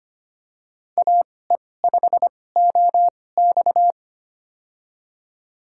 Morse code, heard as a keyed tone, about 25 wpm, spaced as for 18 wpm, 700 Hz. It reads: AE5OX
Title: USA Call Signs - 5 Characters - 2x Character Spacing - 25wpm